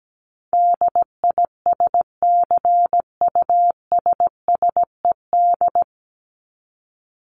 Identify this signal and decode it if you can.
Morse code, heard as a keyed tone, about 17 wpm, 700 Hz. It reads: DISCUSSED
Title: ED Suffix - Top 100 Words with Base Words 17wpm